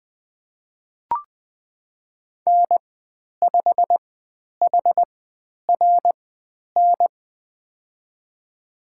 Morse code, heard as a keyed tone, about 20 wpm, spaced as for 10 wpm, 700 Hz. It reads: N5HRN
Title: 095 Lesson 16g - Letter c - Call Signs 20wpm